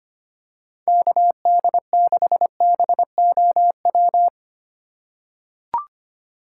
Morse code, heard as a keyed tone, about 25 wpm, 700 Hz. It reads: KD6BOW
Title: USA Call Signs - 2x Character Spacing - 25wpm